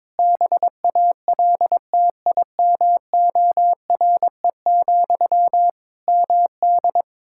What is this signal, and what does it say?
Morse code, heard as a keyed tone, about 22 wpm, 700 Hz. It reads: BALTIMORE, MD
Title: Practice QSO - Exchange 1 - Beginning - 22wpm